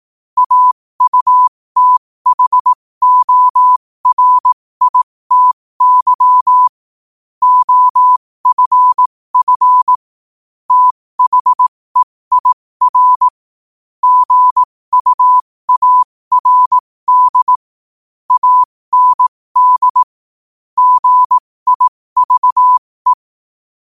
Morse code, heard as a keyed tone, about 18 wpm, spaced as for 15 wpm, 1000 Hz. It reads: AUTHORITY OFF THEIR GUARD AND GIVE
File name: qotdfiles / 15wpm